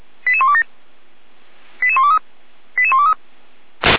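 Five-tone selective call (ZVEI), heard as quick runs of five tones, tones of about 70 ms. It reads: 88127 88123 88123
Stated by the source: Five-tone ZVEI selective call received on an SDR by the Feuerwehr Orlamünde fire brigade server.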